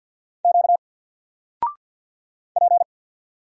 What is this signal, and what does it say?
Morse code, heard as a keyed tone, about 48 wpm, 700 Hz. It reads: Q P
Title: Single Character (Letter:Number:Punctuation:Prosign) Alt 50wpm